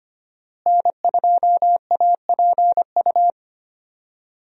Morse code, heard as a keyed tone, about 25 wpm, 700 Hz. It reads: N2APU
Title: USA Call Signs - 5 Characters - Repeated 2x - 25wpm